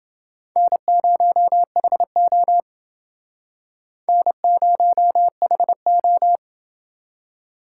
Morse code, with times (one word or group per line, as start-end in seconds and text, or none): 0.6-2.6 N0HO
4.1-6.4 N0HO